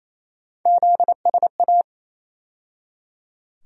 Morse code, heard as a keyed tone, about 27 wpm, 700 Hz. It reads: ZSA
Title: USA Call Sign Postfixes - 3 Letters 28wpm